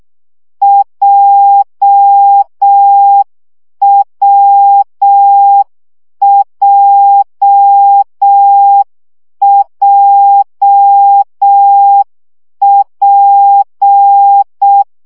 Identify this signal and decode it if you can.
Morse code, heard as a keyed tone, about 6 wpm, 800 Hz. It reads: JWJJP
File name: CW gyakorlas w p j 5 baud